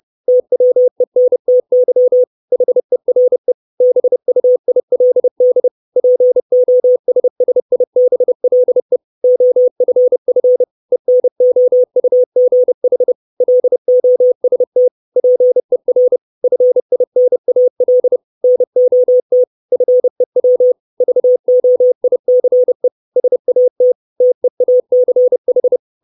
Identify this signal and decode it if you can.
Morse code, heard as a keyed tone, about 30 wpm, 500 Hz. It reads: TWENTY HERE BUILD POSSIBLE OFF ENOUGH LOST PER FINAL NOT FEW VOICE SAT TEACH
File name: 632 Words 30WPM